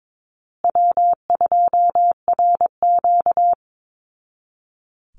Morse code, heard as a keyed tone, about 22 wpm, 700 Hz. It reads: W2RQ
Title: Call Signs Repeated 2x - 22wpm